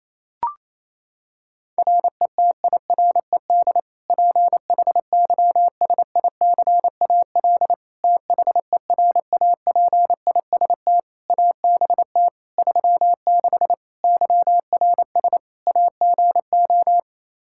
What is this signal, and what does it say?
Morse code, heard as a keyed tone, about 28 wpm, 700 Hz. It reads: RETIRED PHYSICAL THERAPIST ABT 36 YRS AGO
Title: Practice QSO - Exchange 3 - More Info - 28wpm